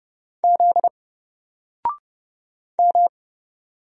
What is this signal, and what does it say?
Morse code, heard as a keyed tone, about 30 wpm, 700 Hz. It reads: Z M
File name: Single-Letter-Number-30wpm